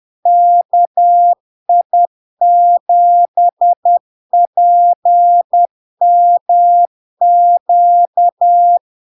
Morse code, heard as a keyed tone, about 10 wpm, 700 Hz. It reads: KI7PMQ
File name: LARK-FD-2022-Log-in-CW-10wpm-700-Hz